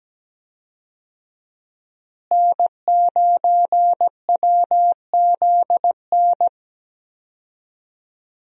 Morse code, heard as a keyed tone, about 17 wpm, 700 Hz. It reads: N9WZN
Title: USA Call Signs - 5 Characters - Repeated 3x - 17wpm